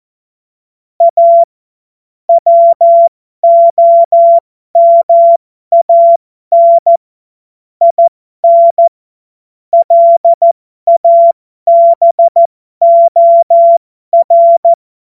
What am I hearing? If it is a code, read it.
Morse code, heard as a keyed tone, about 14 wpm, spaced as for 12 wpm, 680 Hz. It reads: A WOMAN IN LABOR
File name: Morsekurs / Witze